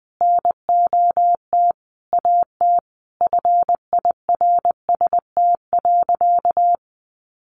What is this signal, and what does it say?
Morse code, heard as a keyed tone, about 20 wpm, 700 Hz. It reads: NOT AT FIRST.
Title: Sentences from Top 900 Words 20wpm